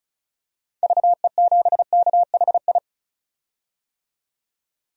Morse code, heard as a keyed tone, about 36 wpm, 700 Hz. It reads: VE7KHI
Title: DXCC-100 Call Signs - 6 Characters - 35wpm